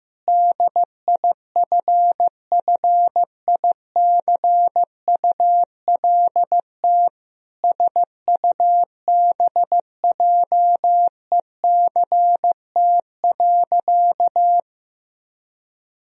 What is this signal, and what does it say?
Morse code, heard as a keyed tone, about 15 wpm, 700 Hz. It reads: DIFFICULT SUBJECT.
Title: Sentences from Top 1000 Words - Review Part 3 - 15wpm